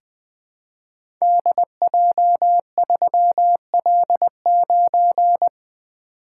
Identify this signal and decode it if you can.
Morse code, heard as a keyed tone, about 20 wpm, 700 Hz. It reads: DJ3L9
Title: Code Groups - 5 Letter:Numbers - 20wpm